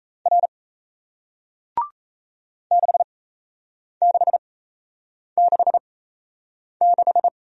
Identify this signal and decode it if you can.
Morse code, heard as a keyed tone, about 38 wpm, spaced as for 8 wpm, 700 Hz. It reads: R 6666